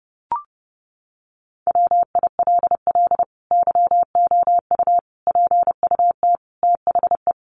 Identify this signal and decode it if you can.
Morse code, heard as a keyed tone, about 30 wpm, 700 Hz. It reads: WILL YOU PUT THE
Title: Sentences from Top 1400 Words 30wpm